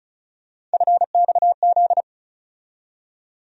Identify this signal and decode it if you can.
Morse code, heard as a keyed tone, about 36 wpm, 700 Hz. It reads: FXZ
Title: USA Call Sign Postfixes - 3 Letters 35wpm